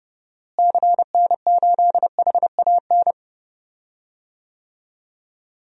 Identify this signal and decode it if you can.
Morse code, heard as a keyed tone, about 30 wpm, 700 Hz. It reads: CN8HAN
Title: DXCC-100 Call Signs - 6 Characters - Repeated 2x - 30wpm